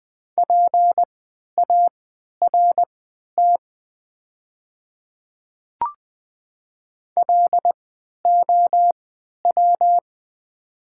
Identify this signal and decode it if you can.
Morse code, heard as a keyed tone, about 20 wpm, spaced as for 11 wpm, 700 Hz. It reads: PART LOW
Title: 3x Character Spacing - Top 300 Words - 20wpm